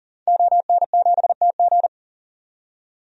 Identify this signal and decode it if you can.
Morse code, heard as a keyed tone, about 40 wpm, 700 Hz. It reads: ON7TG
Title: DXCC-100 Call Signs - 5 Characters - 40wpm